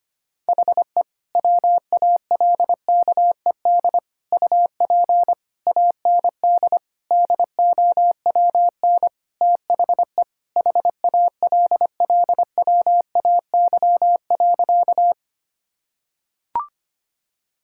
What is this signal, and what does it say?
Morse code, heard as a keyed tone, about 25 wpm, 700 Hz. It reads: HE WALKED UP AND DOWN THE HALLWAY.
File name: practice / 25wpm / Sentences with 4-Word Phrases - 25wpm